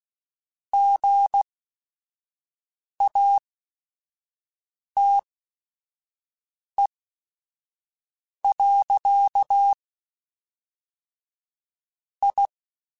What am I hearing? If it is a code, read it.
Morse code, heard as a keyed tone, about 16 wpm, spaced as for 5 wpm, 800 Hz. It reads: GATE. I
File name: Technical Subjects / Technical Subjects-mp3s / AND Gate @5